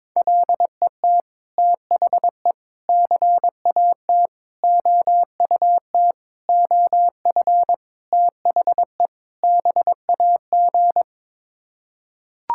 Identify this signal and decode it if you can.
Morse code, heard as a keyed tone, about 21 wpm, 700 Hz. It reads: LET THE CAT OUT OF THE BAG
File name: practice / 22wpm / Sets of 7 Words - 22wpm